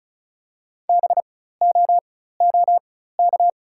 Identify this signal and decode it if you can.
Morse code, heard as a keyed tone, about 36 wpm, spaced as for 16 wpm, 690 Hz. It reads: BOOK
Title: 4x Character Spacing - Top 300 Words - 35wpm